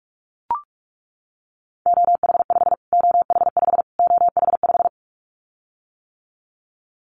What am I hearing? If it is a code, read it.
Morse code, heard as a keyed tone, about 44 wpm, 700 Hz. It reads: OH5 OH5 OH5
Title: Callsign Prefixes for Top 25 DXCC - Repeated 3x - 45wpm